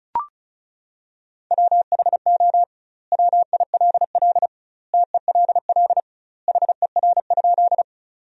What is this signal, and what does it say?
Morse code, heard as a keyed tone, about 36 wpm, 700 Hz. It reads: WHO WILL TELL HER?
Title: 2x Word-Spacing - Sentences from Top 200 Words - 35wpm